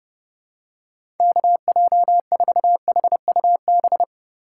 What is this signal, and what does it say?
Morse code, heard as a keyed tone, about 29 wpm, 700 Hz. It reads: KJ4HUB
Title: USA Call Signs - 6 Characters - 6x Character Spacing - 30wpm